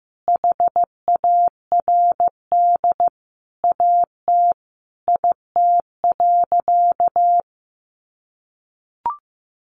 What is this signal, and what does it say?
Morse code, heard as a keyed tone, about 15 wpm, 700 Hz. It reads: HARD AT IT.
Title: Sentences from Top 300 Words 15wpm